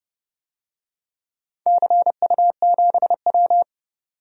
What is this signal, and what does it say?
Morse code, heard as a keyed tone, about 30 wpm, 700 Hz. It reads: CU7W
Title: DXCC-100 Call Signs - 4 Characters - 30wpm